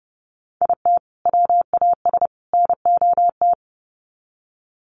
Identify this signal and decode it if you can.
Morse code, heard as a keyed tone, about 30 wpm, 700 Hz. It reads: IT WAS NOT
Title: Sets of 3 Words 30wpm